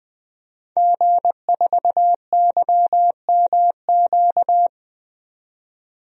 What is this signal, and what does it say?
Morse code, heard as a keyed tone, about 20 wpm, 700 Hz. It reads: G4YMQ
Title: DXCC-100 Call Signs - 5 Characters - 20wpm